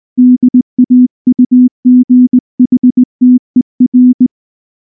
Morse code, heard as a keyed tone, about 19 wpm, 250 Hz. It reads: DAUGHTER
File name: morse / morse-training-20WPM-8